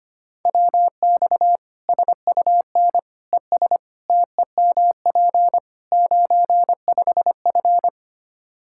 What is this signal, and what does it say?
Morse code, heard as a keyed tone, about 25 wpm, 700 Hz. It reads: WX SUN ES TEMP 95F
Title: K1USN SST QSO Practice 25wpm